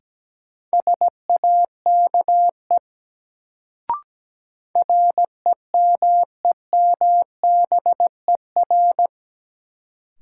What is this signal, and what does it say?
Morse code, heard as a keyed tone, about 17 wpm, 700 Hz. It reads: SAKE REMEMBER
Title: Sets of 4 Words - New Word Drill - 17wpm